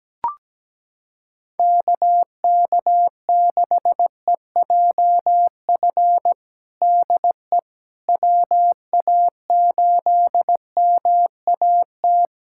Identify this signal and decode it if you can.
Morse code, heard as a keyed tone, about 17 wpm, 700 Hz. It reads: KK6EJF DE WA8MAT
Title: USA Callsign DE Callsign 17wpm